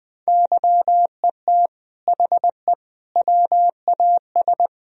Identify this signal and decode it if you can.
Morse code, heard as a keyed tone, about 21 wpm, 700 Hz. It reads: YET HE WAS
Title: Sentences from Top 1000 Words - Review Part 3 - 20wpm